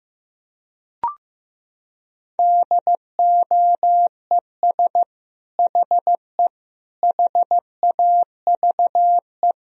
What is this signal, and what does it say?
Morse code, heard as a keyed tone, about 15 wpm, 700 Hz. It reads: DOES HE HAVE
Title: Sentences from Top 1000 Words - Review Part 3 - 15wpm